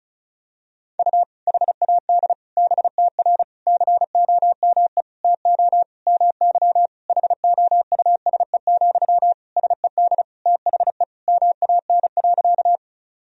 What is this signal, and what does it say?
Morse code, heard as a keyed tone, about 34 wpm, 700 Hz. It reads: U HAD BTR COME TO MY HOUSE, SED THE MAN.
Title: Sentences from Top 400 Words and Abbreviations 35wpm